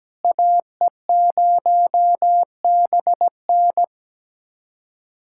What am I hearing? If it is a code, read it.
Morse code, heard as a keyed tone, about 17 wpm, 700 Hz. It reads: AE0BN
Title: USA Callsign DE Callsign - 5 Characters - 17wpm